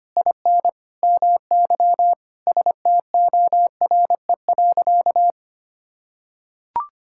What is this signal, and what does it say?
Morse code, heard as a keyed tone, about 25 wpm, 700 Hz. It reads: IN MY STORE.
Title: Sentences from Top 1000 Words - Review Part 5 - 25wpm